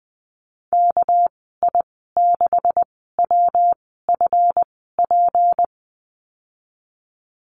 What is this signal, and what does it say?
Morse code, heard as a keyed tone, about 20 wpm, spaced as for 14 wpm, 700 Hz. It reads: KI6WFP
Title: 231 Lesson 37g - Letter x - Call Signs 20wpm